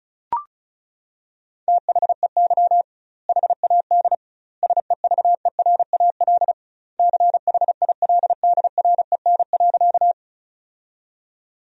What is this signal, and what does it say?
Morse code, heard as a keyed tone, about 35 wpm, 700 Hz. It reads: THEY HAD SEVERAL CHILDREN.
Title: 2x Word-Spacing - Sentences from Top 400 Words - 35wpm